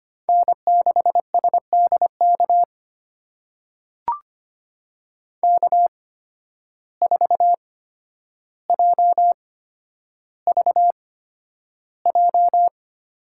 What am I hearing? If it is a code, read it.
Morse code, heard as a keyed tone, about 25 wpm, spaced as for 11 wpm, 700 Hz. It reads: N6SDK K4JVJ
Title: USA Call Signs - 5 Characters - 8x Character Spacing - 25wpm